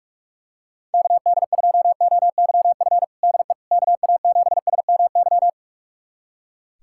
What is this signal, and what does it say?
Morse code, heard as a keyed tone, about 41 wpm, 700 Hz. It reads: KD2OYF DE KA7SMY
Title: USA Callsign DE Callsign - 6 Characters - 45wpm